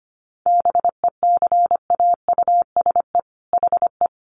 Morse code, heard as a keyed tone, about 25 wpm, 700 Hz. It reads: BECAUSE HE
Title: Sentences from Top 1400 Words 25wpm